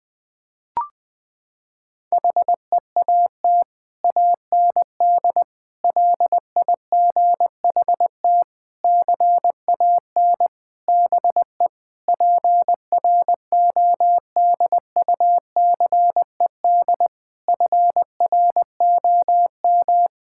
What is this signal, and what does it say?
Morse code, heard as a keyed tone, about 21 wpm, 700 Hz. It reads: HEAT AND LIGHT CAN BE PRODUCED FROM
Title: Sentences from Top 1000 Words - Review Part 6 - 20wpm